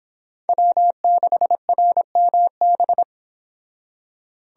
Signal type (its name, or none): Morse code